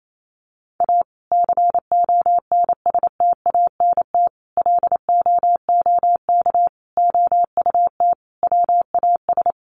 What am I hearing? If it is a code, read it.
Morse code, heard as a keyed tone, about 28 wpm, 700 Hz. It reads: A CONSTANT LOOK OUT WAS
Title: Sentences from Top 1500 Words 28wpm